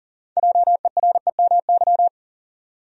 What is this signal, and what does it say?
Morse code, heard as a keyed tone, about 40 wpm, 700 Hz. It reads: JEREMY
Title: USA Common Names - 40wpm